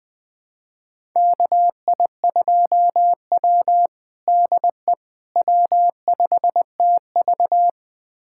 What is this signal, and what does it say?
Morse code, heard as a keyed tone, about 20 wpm, 700 Hz. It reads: KI2W DE W5TV
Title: USA Callsign DE Callsign - 4 Characters - 20wpm